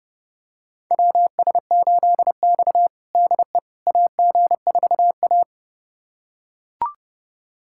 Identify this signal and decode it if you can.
Morse code, heard as a keyed tone, about 30 wpm, 700 Hz. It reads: WS8X DE AG4A
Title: USA Callsign DE Callsign - 4 Characters - 30wpm